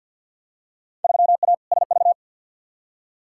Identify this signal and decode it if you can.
Morse code, heard as a keyed tone, about 49 wpm, 700 Hz. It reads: 3A SV
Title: Field Day Exchanges - 50wpm